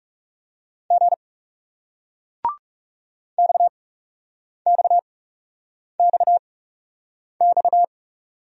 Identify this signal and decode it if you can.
Morse code, heard as a keyed tone, about 40 wpm, 700 Hz. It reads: G X X X X